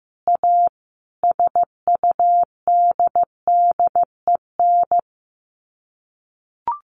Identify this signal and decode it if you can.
Morse code, heard as a keyed tone, about 15 wpm, 700 Hz. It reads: A SUDDEN
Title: Sets of 4 Words 15wpm